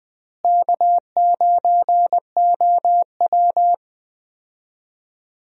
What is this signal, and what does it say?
Morse code, heard as a keyed tone, about 20 wpm, 700 Hz. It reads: K9OW
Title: USA Call Signs - 4 Characters - Repeated 3x - 20wpm